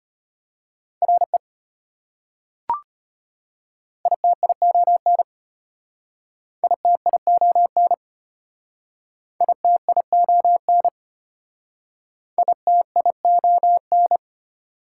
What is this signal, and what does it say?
Morse code, heard as a keyed tone, about 32 wpm, 700 Hz. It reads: RE ITION ITION ITION ITION